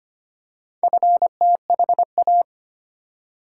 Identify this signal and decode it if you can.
Morse code, heard as a keyed tone, about 25 wpm, 700 Hz. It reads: FTHA
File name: practice / 25wpm / Four Letter-Number Combinations - 25wpm